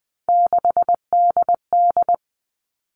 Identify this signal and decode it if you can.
Morse code, heard as a keyed tone, about 20 wpm, 700 Hz. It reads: 6DD
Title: All Cut Numbers - 3-Digit 20wpm